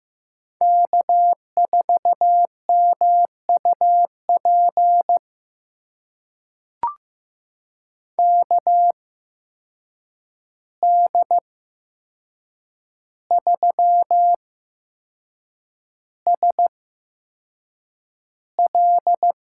Morse code, heard as a keyed tone, about 15 wpm, spaced as for 6 wpm, 700 Hz. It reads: K4MUP KD3SL
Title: USA Call Signs - 5 Characters - 8x Character Spacing - 15wpm